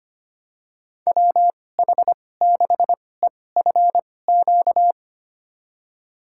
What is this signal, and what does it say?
Morse code, heard as a keyed tone, about 25 wpm, spaced as for 18 wpm, 700 Hz. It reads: WH6EFQ
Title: USA Call Signs - 2x Character Spacing - 25wpm